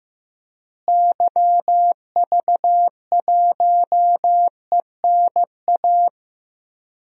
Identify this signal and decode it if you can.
Morse code, heard as a keyed tone, about 15 wpm, 700 Hz. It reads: YV1ENA
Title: DXCC-100 Call Signs - 6 Characters - Repeated 2x - 15wpm